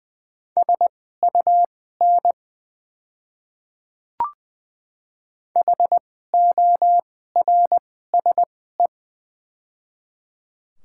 Morse code, heard as a keyed tone, about 20 wpm, spaced as for 14 wpm, 700 Hz. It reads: SUN HORSE
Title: 2x Character Spacing - Top 300 Words - 20wpm